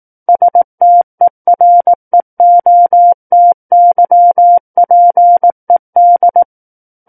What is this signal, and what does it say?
Morse code, heard as a keyed tone, about 18 wpm, 700 Hz. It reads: STEREOTYPED
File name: Book One / 09-Chapter 9 - 18 WPM